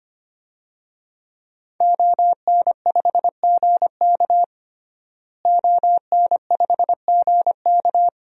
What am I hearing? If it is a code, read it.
Morse code, heard as a keyed tone, about 25 wpm, 700 Hz. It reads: ON5GK ON5GK
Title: DXCC-100 Call Signs - 5 Characters - Repeated 3x - 25wpm